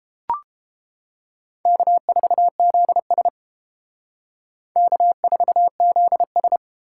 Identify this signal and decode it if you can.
Morse code, heard as a keyed tone, about 29 wpm, 700 Hz. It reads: K4ZS K4ZS